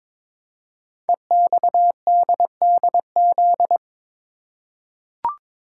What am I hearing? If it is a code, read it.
Morse code, heard as a keyed tone, about 22 wpm, 700 Hz. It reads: EXDDZ
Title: Code Groups - 5 Letters - 22wpm